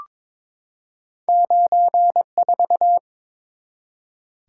Morse code, heard as a keyed tone, about 22 wpm, 700 Hz. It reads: 94
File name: practice / 22wpm / Two Letter-Number Combinations - 22wpm